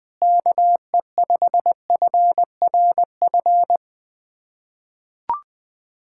Morse code, heard as a keyed tone, about 20 wpm, 700 Hz. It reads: KE5FRF
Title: K1USN SST QSO Practice - 20wpm